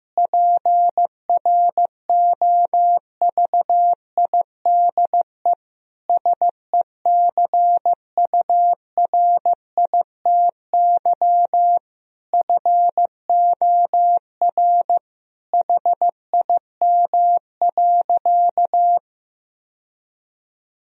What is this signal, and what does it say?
Morse code, heard as a keyed tone, about 15 wpm, 700 Hz. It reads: PROVIDE SECURITY FOR HIM.
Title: Sentences from Top 1000 Words - Review Part 4 - 15wpm